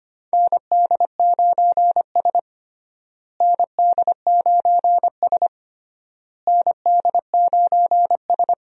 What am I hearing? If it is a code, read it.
Morse code, heard as a keyed tone, about 25 wpm, 700 Hz. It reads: ND9S ND9S ND9S
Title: USA Call Signs - 4 Characters - Repeated 3x - 25wpm